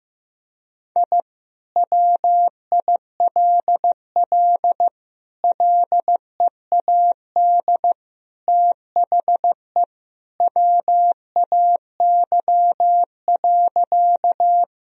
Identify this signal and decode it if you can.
Morse code, heard as a keyed tone, about 16 wpm, 700 Hz. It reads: I WILL LEAD THE WAY.
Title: Sentences from Top 1000 Words - Review Part 2 - 15wpm